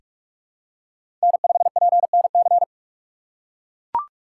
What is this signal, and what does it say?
Morse code, heard as a keyed tone, about 44 wpm, 700 Hz. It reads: N5PNC
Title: USA Call Signs - 5 Characters - 6x Character Spacing - 45wpm